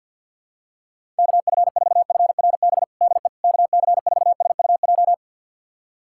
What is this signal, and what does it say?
Morse code, heard as a keyed tone, about 52 wpm, 700 Hz. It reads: KF4FRB DE KC4SUJ
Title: USA Callsign DE Callsign - 6 Characters - 50wpm